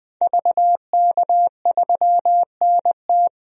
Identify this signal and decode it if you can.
Morse code, heard as a keyed tone, about 20 wpm, 700 Hz. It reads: VK3NT
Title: DXCC-100 Call Signs - 5 Characters - 20wpm